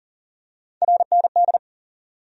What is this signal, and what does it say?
Morse code, heard as a keyed tone, about 40 wpm, 700 Hz. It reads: RND
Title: USA Call Sign Postfixes - 3 Letters 40wpm